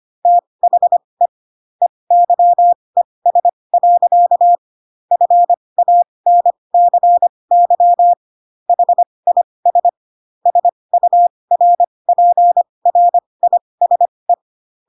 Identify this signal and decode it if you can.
Morse code, heard as a keyed tone, about 25 wpm, spaced as for 20 wpm, 700 Hz. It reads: THE EYES. FANCY HIS SURPRISE
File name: Pinocchio / Pinocchio ch3 15 25 ik1hge